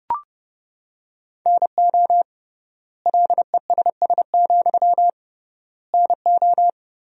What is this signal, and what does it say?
Morse code, heard as a keyed tone, about 30 wpm, 700 Hz. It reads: NO LESS, NO
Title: 3x Word-Spacing - Sentences from Top 400 Words - 30wpm